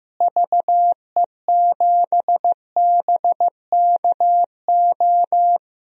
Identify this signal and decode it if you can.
Morse code, heard as a keyed tone, about 15 wpm, 700 Hz. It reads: VE7BKO
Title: DXCC-100 Call Signs - 6 Characters - Repeated 3x - 15wpm